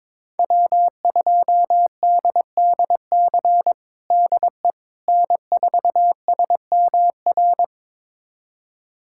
Morse code, heard as a keyed tone, about 22 wpm, 700 Hz. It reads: W2DDC DE N4SMR
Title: USA Callsign DE Callsign - 5 Characters - 22wpm